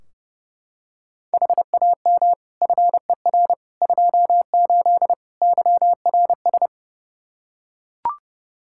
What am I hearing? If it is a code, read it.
Morse code, heard as a keyed tone, about 30 wpm, 700 Hz. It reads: HAM FER 28 YRS
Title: Practice QSO - Exchange 3 - More Info - 30wpm